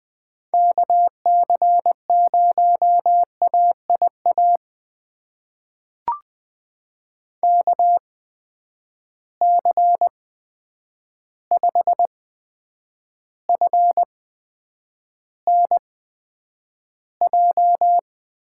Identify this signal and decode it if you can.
Morse code, heard as a keyed tone, about 20 wpm, spaced as for 9 wpm, 700 Hz. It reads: KC0AIA KC5FNJ
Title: USA Call Signs - 6 Characters - 8x Character Spacing - 20wpm